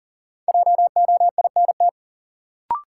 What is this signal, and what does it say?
Morse code, heard as a keyed tone, about 40 wpm, 700 Hz. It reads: JOINT
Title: Sentences from Top 1300 Words - New Word Drill Alt - 40wpm